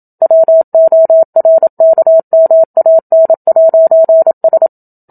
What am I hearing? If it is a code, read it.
Morse code, heard as a keyed tone, about 27 wpm, 650 Hz. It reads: WORKMAN'S
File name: Book Two / 01-Chapter 1 - 27 WPM